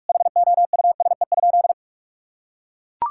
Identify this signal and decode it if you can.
Morse code, heard as a keyed tone, about 41 wpm, 700 Hz. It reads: HOUSE?
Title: Sentences from Top 1000 Words - Review Part 5 - 45wpm